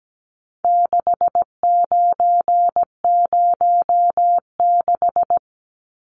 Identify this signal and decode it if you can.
Morse code, heard as a keyed tone, about 17 wpm, 700 Hz. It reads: 6906
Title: Numbers - 4-Digit 17wpm